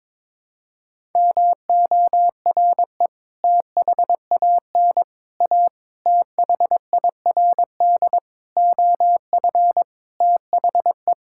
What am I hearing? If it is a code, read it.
Morse code, heard as a keyed tone, about 22 wpm, 700 Hz. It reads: MORE THAN A THIRD OF THE
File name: practice / 22wpm / Sets of 6 Words - 22wpm